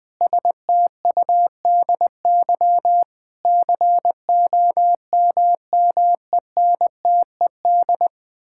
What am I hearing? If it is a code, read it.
Morse code, heard as a keyed tone, about 21 wpm, 700 Hz. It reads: STUDY COMMENTED
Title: Sentences with 7-Word Phrases - Spoken Beginning - 20wpm